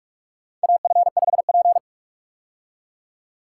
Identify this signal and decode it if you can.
Morse code, heard as a keyed tone, about 47 wpm, 700 Hz. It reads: AF5P
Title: K1USN SST QSO Practice - 45wpm